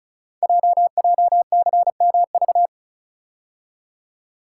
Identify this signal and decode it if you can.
Morse code, heard as a keyed tone, about 36 wpm, 690 Hz. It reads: JJCMV
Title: Code Groups - 5 Letters - 35wpm